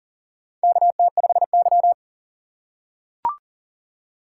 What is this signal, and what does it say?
Morse code, heard as a keyed tone, about 40 wpm, 700 Hz. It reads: KT5Y
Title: USA Call Signs - 4 Characters - 2x Character Spacing - 40wpm